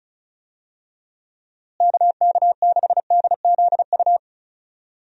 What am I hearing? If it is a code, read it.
Morse code, heard as a keyed tone, about 34 wpm, 700 Hz. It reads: KK6DZU
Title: USA Call Signs - 6 Characters - Repeated 3x - 35wpm